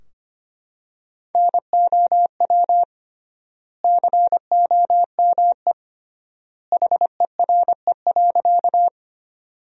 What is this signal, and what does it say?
Morse code, heard as a keyed tone, about 25 wpm, 700 Hz. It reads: NOW COME HERE.
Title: 3x Word-Spacing - Sentences from Top 200 Words - 25wpm